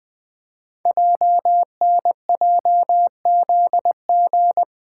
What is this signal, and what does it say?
Morse code, heard as a keyed tone, about 20 wpm, 700 Hz. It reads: JNJZG
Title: Code Groups - 5 Letters - 20wpm